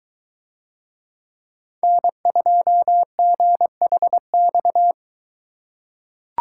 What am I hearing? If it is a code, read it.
Morse code, heard as a keyed tone, about 22 wpm, 700 Hz. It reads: N2GHX